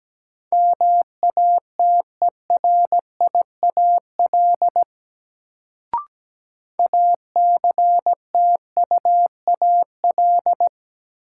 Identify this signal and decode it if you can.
Morse code, heard as a keyed tone, about 17 wpm, 700 Hz. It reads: MATERIAL ACTUAL
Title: Sentences from Top 1000 Words - Review Part 5 - New Word Drill - 17wpm